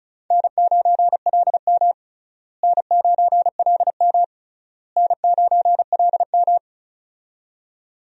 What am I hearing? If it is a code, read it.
Morse code, heard as a keyed tone, about 36 wpm, 700 Hz. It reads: N9LM N9LM N9LM
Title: USA Call Signs - 4 Characters - Repeated 3x - 35wpm